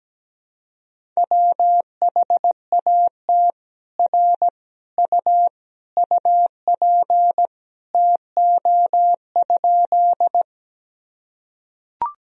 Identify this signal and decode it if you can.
Morse code, heard as a keyed tone, about 17 wpm, 700 Hz. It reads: WHAT R U UP TO?
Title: Sentences from Top 100 Words and Abbreviations 17wpm